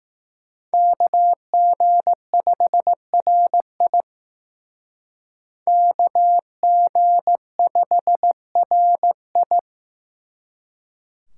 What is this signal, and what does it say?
Morse code, heard as a keyed tone, about 19 wpm, 700 Hz. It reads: KG5RI KG5RI
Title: Speed Racing - USA 2x2 Call Signs 15wpm